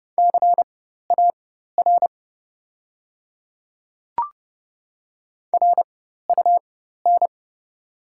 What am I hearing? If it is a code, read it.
Morse code, heard as a keyed tone, about 30 wpm, spaced as for 14 wpm, 690 Hz. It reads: CAR RUN
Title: 4x Character Spacing - Top 300 Words - 30wpm